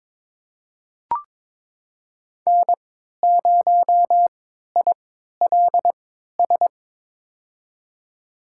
Morse code, heard as a keyed tone, about 22 wpm, spaced as for 13 wpm, 700 Hz. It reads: N0ILS
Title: USA Call Signs - 5 Characters - 3x Character Spacing - 22wpm